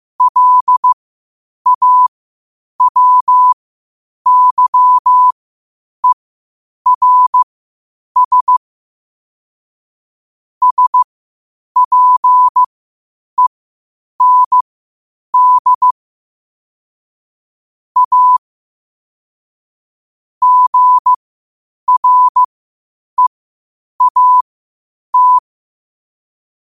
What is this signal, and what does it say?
Morse code, heard as a keyed tone, about 15 wpm, spaced as for 8 wpm, 1000 Hz. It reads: LAWYERS SPEND A GREAT
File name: qotdfiles / 8wpm